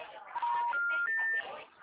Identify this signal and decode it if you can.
Special information tone, with speech.